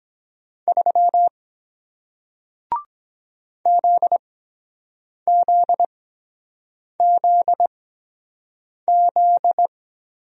Morse code, heard as a keyed tone, about 23 wpm, spaced as for 7 wpm, 700 Hz. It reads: 3 ZZZZ